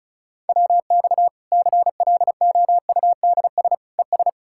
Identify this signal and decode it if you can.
Morse code, heard as a keyed tone, about 36 wpm, 700 Hz. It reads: WX CLOUDS ES
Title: Practice QSO - Exchange 2 - Info - 35wpm